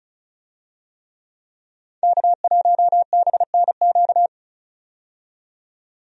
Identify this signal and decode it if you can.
Morse code, heard as a keyed tone, about 36 wpm, 700 Hz. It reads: K1BNQ
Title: USA Call Signs - 5 Characters - 35wpm